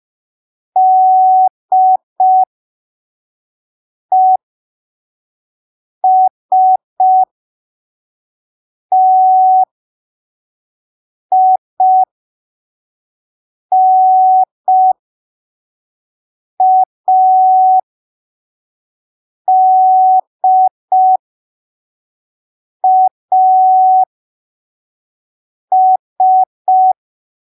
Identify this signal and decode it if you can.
Morse code, heard as a keyed tone, about 5 wpm, spaced as for 3 wpm, 750 Hz. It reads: DESTINADAS
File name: arquivos / texto / texto 01 5ppm